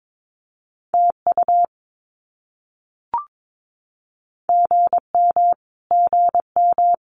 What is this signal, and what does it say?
Morse code, heard as a keyed tone, about 22 wpm, 700 Hz. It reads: TU GM GM
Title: Abbreviations Repeated 3x - 22wpm